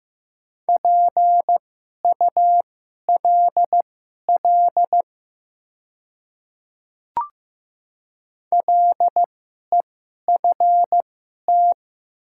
Morse code, heard as a keyed tone, about 15 wpm, spaced as for 11 wpm, 700 Hz. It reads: PULL LEFT
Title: 2x Character Spacing - Top 500 Words - 15wpm